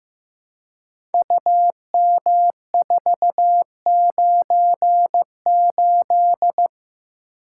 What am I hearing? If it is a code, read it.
Morse code, heard as a keyed tone, about 15 wpm, 700 Hz. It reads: UM498
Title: Code Groups - 5 Letter:Numbers - 15wpm